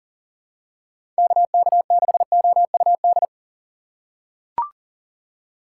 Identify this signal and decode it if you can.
Morse code, heard as a keyed tone, about 40 wpm, 700 Hz. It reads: KK6OUD
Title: USA Call Signs - 6 Characters - 4x Character Spacing - 40wpm